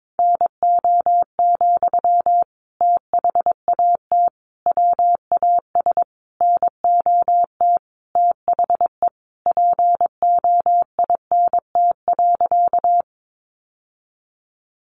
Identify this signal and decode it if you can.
Morse code, heard as a keyed tone, about 22 wpm, 700 Hz. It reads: NO, THAT WAS NOT THE POINT.
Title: Sentences from Top 600 Words 22wpm